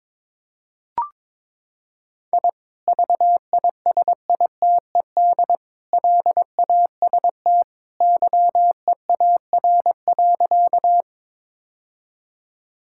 Morse code, heard as a keyed tone, about 22 wpm, 700 Hz. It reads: I VISITED LAST YEAR.
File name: practice / 22wpm / Sentences with 2-Word Phrases - 22wpm